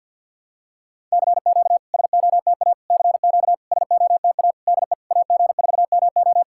Morse code, heard as a keyed tone, about 50 wpm, 700 Hz. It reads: CQ SOTA CQ SOTA DE AG4MO
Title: SOTA QSO Practice 50wpm